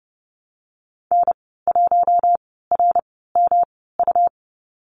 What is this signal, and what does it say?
Morse code, heard as a keyed tone, about 30 wpm, spaced as for 17 wpm, 700 Hz. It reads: N1RMU
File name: Lesson 19g - Letter w - Call Signs 30wpm